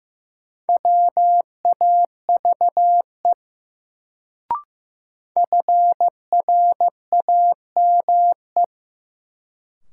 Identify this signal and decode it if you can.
Morse code, heard as a keyed tone, about 15 wpm, 700 Hz. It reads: WAVE FRAME
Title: Sentences from Top 1400 Words - New Word Drill Alt - 15wpm